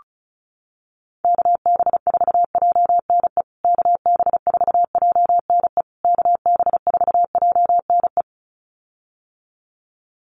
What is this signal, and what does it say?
Morse code, heard as a keyed tone, about 34 wpm, 700 Hz. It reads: KB4JNE KB4JNE KB4JNE
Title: USA Call Signs Repeated 3x 35wpm